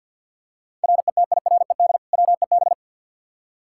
Morse code, heard as a keyed tone, about 50 wpm, 700 Hz. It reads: RETIRED WEB
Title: K1USN SST QSO Practice 50wpm